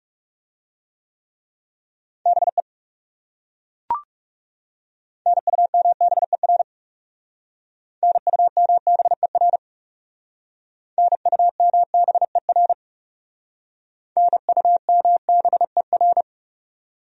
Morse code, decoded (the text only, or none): BE NUMBER NUMBER NUMBER NUMBER